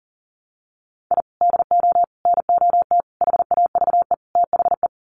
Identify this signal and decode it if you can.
Morse code, heard as a keyed tone, about 40 wpm, 700 Hz. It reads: I DO NOT HAVE THE
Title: Sets of 5 Words 40wpm